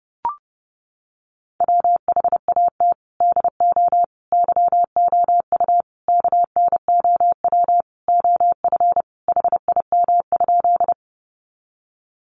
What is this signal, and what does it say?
Morse code, heard as a keyed tone, about 30 wpm, 700 Hz. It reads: WHAT DO YOU KNOW OF HIM?
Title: Sentences from Top 100 Words 30wpm